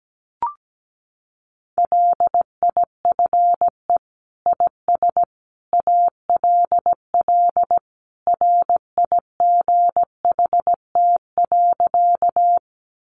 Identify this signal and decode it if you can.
Morse code, heard as a keyed tone, about 17 wpm, 700 Hz. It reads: LIFE IS ALL RIGHT.
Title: Sentences from Top 300 Words 17wpm